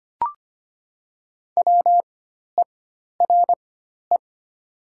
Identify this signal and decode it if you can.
Morse code, heard as a keyed tone, about 25 wpm, spaced as for 12 wpm, 700 Hz. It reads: WERE
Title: 4x Character Spacing - Top 300 Words - 25wpm